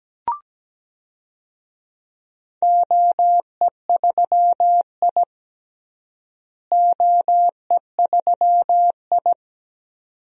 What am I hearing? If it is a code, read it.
Morse code, heard as a keyed tone, about 17 wpm, 700 Hz. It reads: OE3I OE3I
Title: DXCC-100 Call Signs - 4 Characters - Repeated 2x - 17wpm